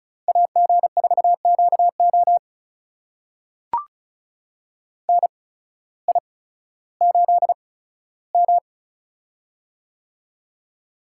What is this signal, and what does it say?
Morse code, heard as a keyed tone, about 34 wpm, spaced as for 16 wpm, 700 Hz. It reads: AG4QO NI8M
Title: USA Call Signs - 8x Character Spacing - 35wpm